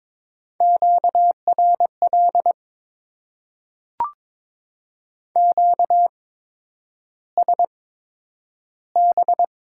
Morse code, heard as a keyed tone, about 21 wpm, spaced as for 9 wpm, 700 Hz. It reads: QRL QSB
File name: Top 90 QSO Elements - 8x Character Spacing - 22wpm